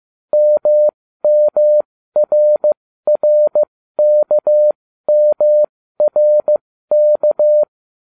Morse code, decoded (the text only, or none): MMRRKMRK